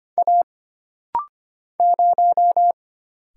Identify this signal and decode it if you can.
Morse code, heard as a keyed tone, about 25 wpm, 700 Hz. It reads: A 0
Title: Single Letter-Number Alt - Rapid-Fire 25wpm